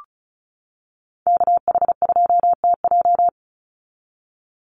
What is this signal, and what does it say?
Morse code, heard as a keyed tone, about 34 wpm, 700 Hz. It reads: KH2TJ
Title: NA SOTA Activator Callsigns 35wpm